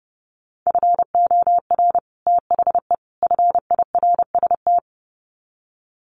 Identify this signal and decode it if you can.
Morse code, heard as a keyed tone, about 28 wpm, 700 Hz. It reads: FOR THE FIRST
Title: Sets of 3 Words 30wpm